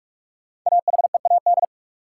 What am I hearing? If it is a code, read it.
Morse code, heard as a keyed tone, about 46 wpm, 700 Hz. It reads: AHEAD
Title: Sentences from Top 1000 Words - Review Part 6 - New Word Drill - 45wpm